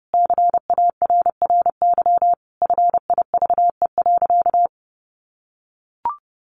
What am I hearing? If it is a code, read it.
Morse code, heard as a keyed tone, about 30 wpm, 700 Hz. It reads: CARRY FIVE.
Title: Sentences from Top 400 Words 30wpm